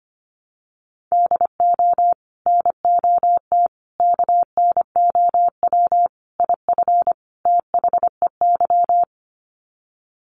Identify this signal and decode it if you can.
Morse code, heard as a keyed tone, about 25 wpm, 700 Hz. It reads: DO NOT KNOW IF THEY
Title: Sets of 5 Words Encore 25wpm